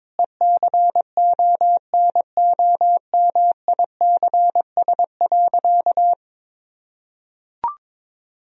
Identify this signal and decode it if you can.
Morse code, heard as a keyed tone, about 22 wpm, 700 Hz. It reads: ECONOMICS.
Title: Sentences from Top 1000 Words - Review Part 3 - 22wpm